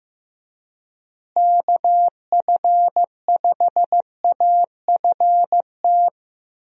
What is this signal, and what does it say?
Morse code, heard as a keyed tone, about 15 wpm, 700 Hz. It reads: KF5AFT
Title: USA Call Signs - 6 Characters - Repeated 3x - 15wpm